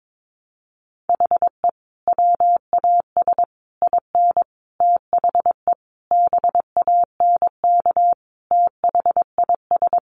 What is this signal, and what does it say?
Morse code, heard as a keyed tone, about 22 wpm, 700 Hz. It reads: HE WAS IN THE BANK THIS
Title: Sentences from Top 900 Words 22wpm